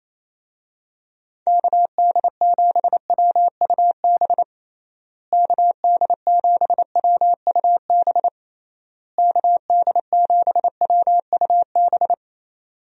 Morse code, heard as a keyed tone, about 28 wpm, 700 Hz. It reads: KD7WUB KD7WUB KD7WUB
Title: USA Call Signs - 6 Characters - Repeated 3x - 28wpm